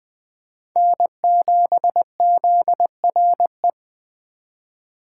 Morse code, heard as a keyed tone, about 20 wpm, 700 Hz. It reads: N7ZRE
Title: USA Call Signs - 5 Characters - 4x Character Spacing - 20wpm